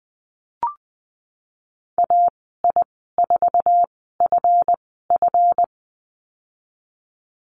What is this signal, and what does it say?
Morse code, heard as a keyed tone, about 20 wpm, spaced as for 15 wpm, 700 Hz. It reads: AI4FF
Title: 186 Lesson 30g - Letter b - Call Signs 20wpm